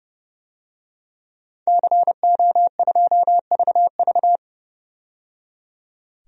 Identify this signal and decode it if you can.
Morse code, heard as a keyed tone, about 30 wpm, 700 Hz. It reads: CO2VV
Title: DXCC-100 Call Signs - 5 Characters - 30wpm